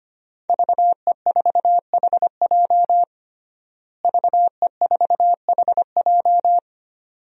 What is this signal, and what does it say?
Morse code, heard as a keyed tone, about 25 wpm, 700 Hz. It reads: VE4HJ VE4HJ
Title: DXCC-100 Call Signs - 5 Characters - Repeated 2x - 25wpm